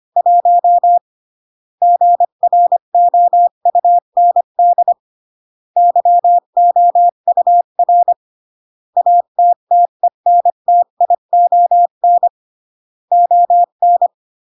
Morse code, heard as a keyed tone, about 25 wpm, 700 Hz. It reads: 1 GROUND YOUR ATTENTION ON
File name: CW / jodorowsky / 82 commandments 25-20